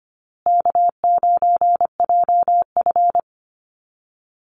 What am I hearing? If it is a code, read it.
Morse code, heard as a keyed tone, about 25 wpm, 700 Hz. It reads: K9JF
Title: Easy Call Signs Repeated 3x - 25wpm